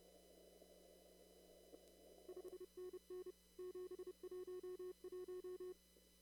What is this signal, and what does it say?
Morse code, heard as a keyed tone, about 30 wpm, 360 Hz. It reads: 5NN 711
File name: CW / kt5j wpx cw-op-k5pi